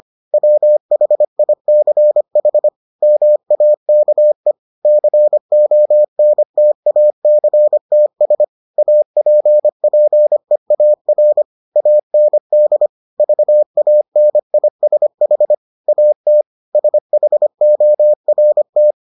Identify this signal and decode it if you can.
Morse code, heard as a keyed tone, about 25 wpm, 600 Hz. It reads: WHICH MAKE CONTACTS APPEAR AND VANISH AT SHORT